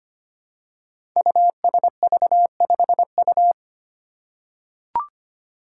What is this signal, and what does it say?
Morse code, heard as a keyed tone, about 25 wpm, 700 Hz. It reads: USV5U
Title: Five Challenging Letter-Number Combinations 25wpm